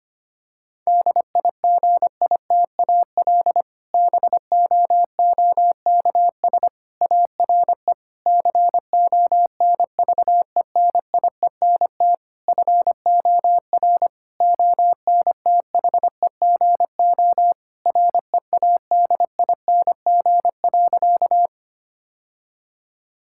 Morse code, heard as a keyed tone, about 25 wpm, 700 Hz. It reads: DIGITAL BOOKS ARE CONVENIENT FOR ONTHEGO READING.
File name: practice / 25wpm / Difficult Sentences with Hints - 25wpm